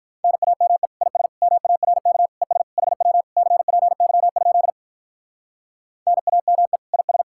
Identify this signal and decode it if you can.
Morse code, heard as a keyed tone, about 44 wpm, 700 Hz. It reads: NAME IS MARK ES HW CPY? NAME IS